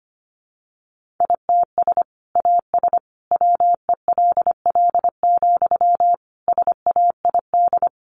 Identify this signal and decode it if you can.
Morse code, heard as a keyed tone, about 25 wpm, 700 Hz. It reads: ITS AS WELL, SAID
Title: Sentences from Top 600 Words 25wpm